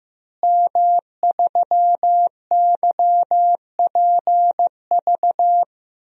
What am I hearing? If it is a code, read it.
Morse code, heard as a keyed tone, about 15 wpm, 700 Hz. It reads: M3YPV
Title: DXCC-100 Call Signs - 5 Characters - Repeated 2x - 15wpm